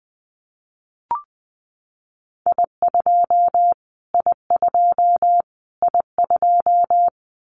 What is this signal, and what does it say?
Morse code, heard as a keyed tone, about 20 wpm, 700 Hz. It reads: I2 I2 I2
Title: Callsign Prefixes for Top 25 DXCC - Repeated 3x - 20wpm